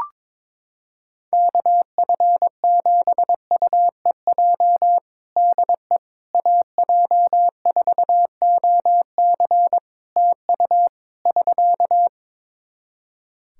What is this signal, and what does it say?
Morse code, heard as a keyed tone, about 22 wpm, 700 Hz. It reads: KF7UEJ DE AJ4OC TU <SK>
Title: Practice QSO - Exchange 4 - Ending - 22wpm